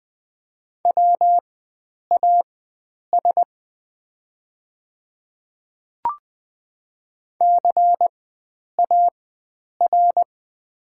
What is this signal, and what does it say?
Morse code, heard as a keyed tone, about 20 wpm, spaced as for 9 wpm, 690 Hz. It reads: WAS CAR
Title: 4x Character Spacing - Top 300 Words - 20wpm